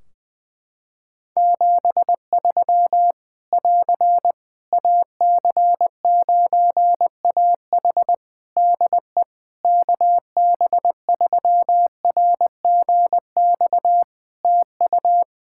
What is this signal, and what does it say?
Morse code, heard as a keyed tone, about 20 wpm, 700 Hz. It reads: 73 <AR> AC9AH DE KB3RGX TU
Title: Practice QSO - Exchange 4 - Ending - 20wpm